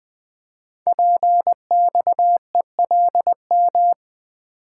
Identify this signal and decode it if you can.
Morse code, heard as a keyed tone, about 20 wpm, 700 Hz. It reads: PXELM
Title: Code Groups - 5 Letters - 20wpm